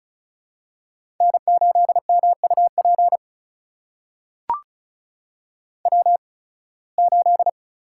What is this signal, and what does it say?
Morse code, heard as a keyed tone, about 36 wpm, spaced as for 23 wpm, 700 Hz. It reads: N8MUP W8
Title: USA Call Signs - 5 Characters - 8x Character Spacing - 35wpm